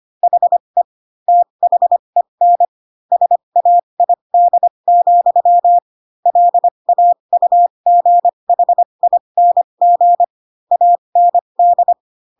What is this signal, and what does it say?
Morse code, heard as a keyed tone, about 25 wpm, spaced as for 22 wpm, 700 Hz. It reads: HE THEN SAID, LAUGHING AND